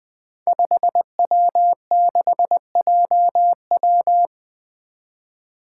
Morse code, heard as a keyed tone, about 20 wpm, 700 Hz. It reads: 5W6JW
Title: Five Challenging Letter-Number Combinations 20wpm